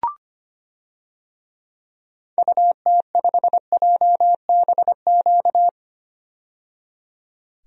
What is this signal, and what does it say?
Morse code, heard as a keyed tone, about 25 wpm, 700 Hz. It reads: UT5JBQ
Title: DXCC-100 Call Signs - 6 Characters - 25wpm